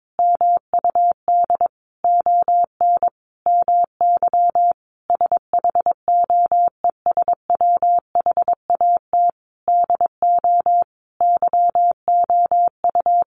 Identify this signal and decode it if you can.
Morse code, heard as a keyed tone, about 22 wpm, 700 Hz. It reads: MUD ON MY SHOES WHAT DO YOU
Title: A Sound Of Thunder 22wpm